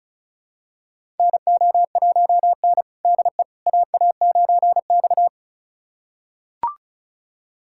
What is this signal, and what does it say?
Morse code, heard as a keyed tone, about 36 wpm, 700 Hz. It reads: NO1N DE AA9X
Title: USA Callsign DE Callsign - 4 Characters - 35wpm